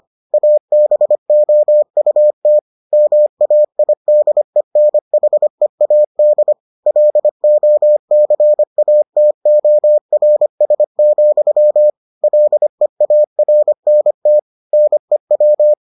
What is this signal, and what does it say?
Morse code, heard as a keyed tone, about 25 wpm, 600 Hz. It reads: ABOUT MAIDENHEAD LOCATORS, LEARNT NEW